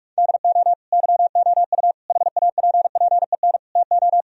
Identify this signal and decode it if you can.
Morse code, heard as a keyed tone, about 41 wpm, 700 Hz. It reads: DO YOU HAPPEN TO
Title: Sentences from Top 1000 Words - Review Part 2 - 45wpm